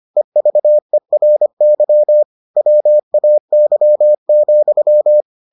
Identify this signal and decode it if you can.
Morse code, heard as a keyed tone, about 25 wpm, 600 Hz. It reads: EVERY WAY,